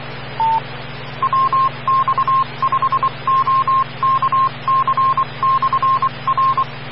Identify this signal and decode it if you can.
Morse code, heard as a keyed tone, about 24 wpm, 1060 Hz, with noise. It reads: WX5OKC/R
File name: RR Skywarn / WX5OKC Amateur Repeater Hz